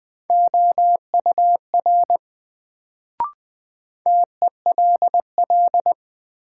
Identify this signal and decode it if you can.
Morse code, heard as a keyed tone, about 20 wpm, 700 Hz. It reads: OUR TELL
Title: Sets of 2 Words - New Word Drill - 20wpm